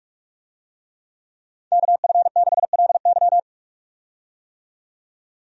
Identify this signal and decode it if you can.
Morse code, heard as a keyed tone, about 46 wpm, 700 Hz. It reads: KF6LY
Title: USA Call Signs - 5 Characters - 45wpm